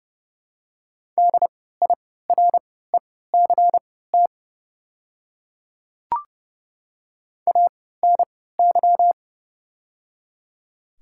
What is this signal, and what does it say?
Morse code, heard as a keyed tone, about 30 wpm, spaced as for 17 wpm, 700 Hz. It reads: DIRECT ANY
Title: 3x Character Spacing - Top 500 Words - 30wpm